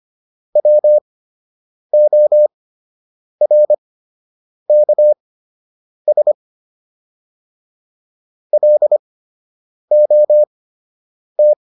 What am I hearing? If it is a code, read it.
Morse code, heard as a keyed tone, about 25 wpm, spaced as for 8 wpm, 600 Hz. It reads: WORKS LOT